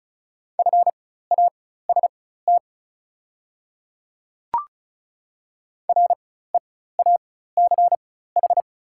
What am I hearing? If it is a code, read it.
Morse code, heard as a keyed tone, about 34 wpm, spaced as for 16 wpm, 690 Hz. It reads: FAST REACH
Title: 4x Character Spacing - Top 500 Words - 35wpm